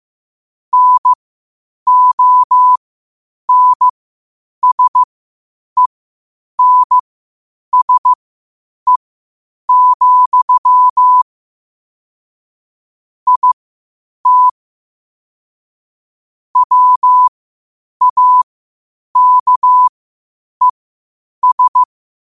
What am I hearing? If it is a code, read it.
Morse code, heard as a keyed tone, about 15 wpm, spaced as for 8 wpm, 1000 Hz. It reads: NONSENSE, IT WAKES
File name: qotdfiles / 8wpm